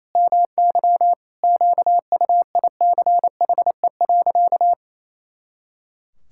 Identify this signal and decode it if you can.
Morse code, heard as a keyed tone, about 28 wpm, 700 Hz. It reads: MY QUICHE.
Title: Pangram Challenge v2 - 28wpm